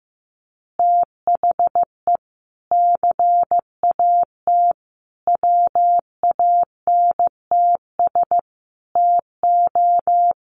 Morse code, heard as a keyed tone, about 15 wpm, 700 Hz. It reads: THE CAT WANTS TO
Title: Additional Easy Phrases 15wpm